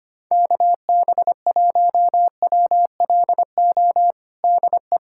Morse code, heard as a keyed tone, about 25 wpm, 700 Hz. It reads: KB1WLO DE
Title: USA Callsign DE Callsign 25wpm